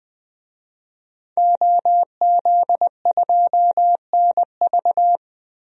Morse code, heard as a keyed tone, about 20 wpm, 700 Hz. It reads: OZ2NV
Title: DXCC-100 Call Signs - 5 Characters - 20wpm